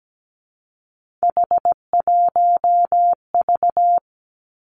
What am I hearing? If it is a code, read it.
Morse code, heard as a keyed tone, about 17 wpm, 700 Hz. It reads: H1V
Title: Three Challenging Letter-Number Combinations 17wpm